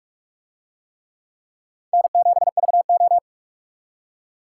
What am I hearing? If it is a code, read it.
Morse code, heard as a keyed tone, about 42 wpm, 700 Hz. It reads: N7VO